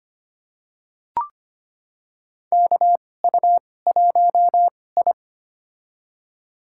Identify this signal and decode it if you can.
Morse code, heard as a keyed tone, about 25 wpm, spaced as for 18 wpm, 700 Hz. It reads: KU1I
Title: USA Call Signs - 2x Character Spacing - 25wpm